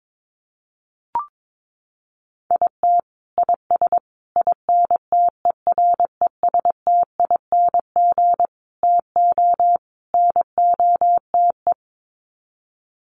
Morse code, heard as a keyed tone, about 22 wpm, 700 Hz. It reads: IT IS INTERESTING TO NOTE
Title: Sets of 5 Words Encore2 22wpm